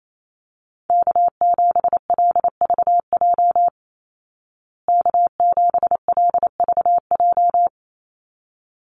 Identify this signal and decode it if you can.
Morse code, heard as a keyed tone, about 28 wpm, 700 Hz. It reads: K7LVJ K7LVJ
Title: Call Signs Repeated 3x - 28wpm